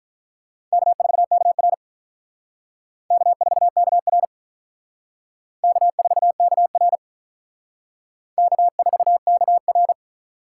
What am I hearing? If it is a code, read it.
Morse code, heard as a keyed tone, about 45 wpm, 700 Hz. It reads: K4KR K4KR K4KR K4KR